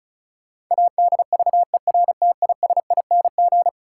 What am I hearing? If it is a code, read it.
Morse code, heard as a keyed tone, about 36 wpm, 700 Hz. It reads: ADVERTISING
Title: Sentences from Top 1000 Words - Review Part 6 - New Word Drill - 35wpm